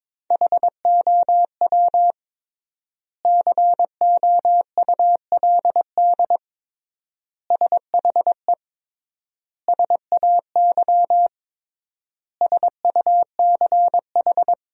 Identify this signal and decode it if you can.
Morse code, heard as a keyed tone, about 22 wpm, 700 Hz. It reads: HOW COULD SHE SAY SUCH
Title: 3x Word-Spacing - Sentences from Top 200 Words - 22wpm